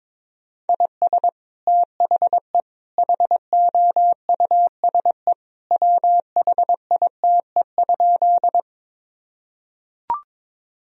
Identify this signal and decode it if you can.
Morse code, heard as a keyed tone, about 22 wpm, 700 Hz. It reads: IS THE HOUSE WHITE?
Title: Sentences from Top 1000 Words - Review Part 1 - 22wpm